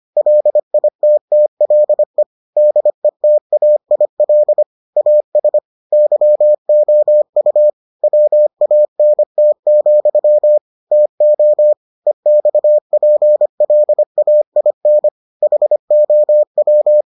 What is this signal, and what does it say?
Morse code, heard as a keyed tone, about 25 wpm, 600 Hz. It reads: LITTLE DETAIL AS YOU WANT, TO EXPLAIN HOW